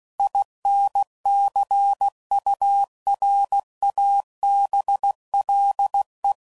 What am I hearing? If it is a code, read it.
Morse code, heard as a keyed tone, about 16 wpm, 800 Hz. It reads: INCURABLE
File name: Sayings / Quotes / Optimist